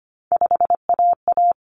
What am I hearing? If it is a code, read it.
Morse code, heard as a keyed tone, about 25 wpm, 700 Hz. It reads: 5AA
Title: Common Cut Numbers - 3-Digit 25wpm